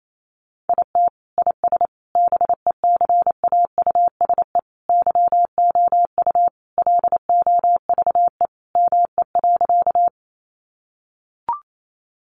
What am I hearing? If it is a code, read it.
Morse code, heard as a keyed tone, about 28 wpm, 700 Hz. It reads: IT IS BECAUSE YOU LOVE ME.
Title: Sentences from Top 600 Words 28wpm